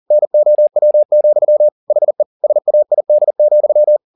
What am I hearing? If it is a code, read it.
Morse code, heard as a keyed tone, about 40 wpm, 600 Hz. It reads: NOW, HE SAID,